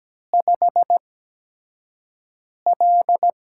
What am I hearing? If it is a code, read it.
Morse code, heard as a keyed tone, about 17 wpm, spaced as for 5 wpm, 700 Hz. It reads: 5L
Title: USA Call Signs - 8x Character Spacing - 17wpm